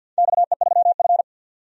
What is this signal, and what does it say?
Morse code, heard as a keyed tone, about 45 wpm, 700 Hz. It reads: XE3F
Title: DXCC-100 Call Signs - 4 Characters - 50wpm